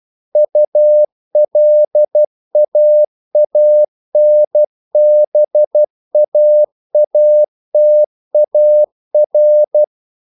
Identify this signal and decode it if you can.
Morse code, heard as a keyed tone, about 12 wpm, 600 Hz. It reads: ULAANBAATAR
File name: CW / Practice / NationalCapitals12